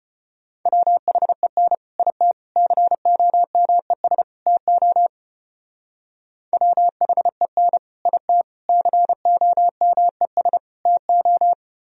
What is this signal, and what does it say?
Morse code, heard as a keyed tone, about 31 wpm, 700 Hz. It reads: WHEN IT COMES TO WHEN IT COMES TO
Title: Speed-Racing - Sets of 4 Words 25wpm